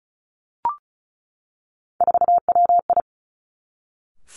Morse code, heard as a keyed tone, about 36 wpm, 700 Hz. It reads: 4WI
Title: Three Challenging Letter-Number Combinations 35wpm